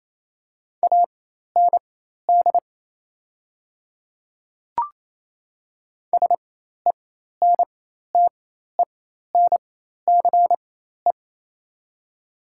Morse code, heard as a keyed tone, about 28 wpm, spaced as for 13 wpm, 690 Hz. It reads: AND SENTENCE
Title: 4x Character Spacing - Top 200 Words - 28wpm